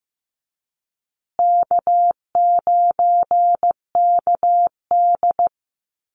Morse code, heard as a keyed tone, about 15 wpm, 700 Hz. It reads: K9KD
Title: USA Call Signs Repeated 3x 15wpm